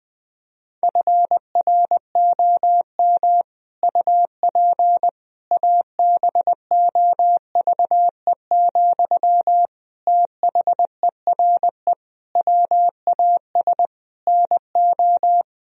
Morse code, heard as a keyed tone, about 20 wpm, 700 Hz. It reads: FROM UP ABOVE, THERE WAS NO
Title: Sentences from Top 1000 Words - Review Part 2 - 20wpm